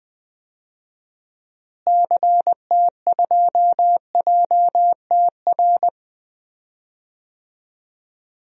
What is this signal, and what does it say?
Morse code, heard as a keyed tone, about 21 wpm, 700 Hz. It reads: CT2JTR
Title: DXCC-100 Call Signs - 6 Characters - Repeated 3x - 20wpm